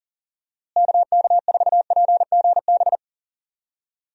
Morse code, heard as a keyed tone, about 40 wpm, 700 Hz. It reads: KK4PGB
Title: USA Call Signs - 6 Characters - Repeated 2x - 40wpm